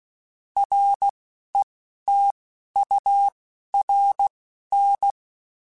Morse code, heard as a keyed tone, about 16 wpm, spaced as for 11 wpm, 800 Hz. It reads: RETURN
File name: (spaced) / QSOs Spaced-mp3s / kc7haa @15 sp